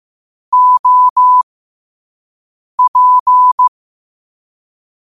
Morse code, heard as a keyed tone, about 15 wpm, spaced as for 5 wpm, 1000 Hz. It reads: OP
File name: qotdfiles / 5wpm